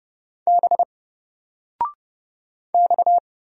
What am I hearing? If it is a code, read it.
Morse code, heard as a keyed tone, about 30 wpm, 700 Hz. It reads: B X
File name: Single-Letter-Number-30wpm